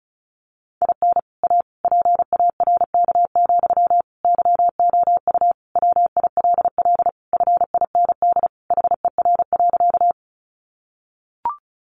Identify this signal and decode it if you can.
Morse code, heard as a keyed tone, about 34 wpm, 700 Hz. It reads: IN A PARK, YOU WILL FIND HER.
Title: Sentences from Top 1300 Words 35wpm